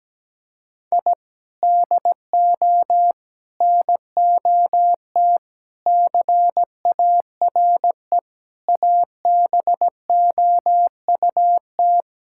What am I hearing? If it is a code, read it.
Morse code, heard as a keyed tone, about 17 wpm, 700 Hz. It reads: I DO NOT CARE ABOUT
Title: Sentences from Top 1000 Words - Review Part 5 - 17wpm